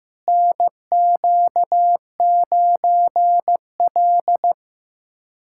Morse code, heard as a keyed tone, about 15 wpm, 700 Hz. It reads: NQ9L
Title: USA Callsign DE Callsign - 4 Characters - 15wpm